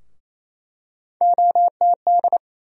Morse code, heard as a keyed tone, about 28 wpm, 700 Hz. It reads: OTD
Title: USA Call Sign Postfixes - 3 Letters 28wpm